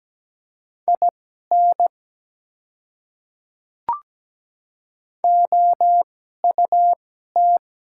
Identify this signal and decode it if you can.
Morse code, heard as a keyed tone, about 17 wpm, spaced as for 12 wpm, 700 Hz. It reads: IN OUT
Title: 2x Character Spacing - Top 100 Words - 17wpm